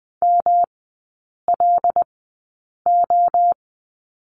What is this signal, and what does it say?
Morse code, heard as a keyed tone, about 20 wpm, spaced as for 8 wpm, 700 Hz. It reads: MLO
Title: 107 Lesson 18e - Letter m - Three Characters 20wpm